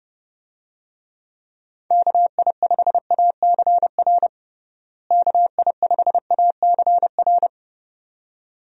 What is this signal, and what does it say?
Morse code, heard as a keyed tone, about 30 wpm, 700 Hz. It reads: KI5ACR KI5ACR
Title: USA Call Signs - 6 Characters - Repeated 2x - 30wpm